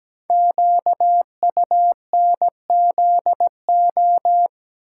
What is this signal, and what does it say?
Morse code, heard as a keyed tone, about 17 wpm, 700 Hz. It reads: QUNZO
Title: Code Groups - 5 Letters - 17wpm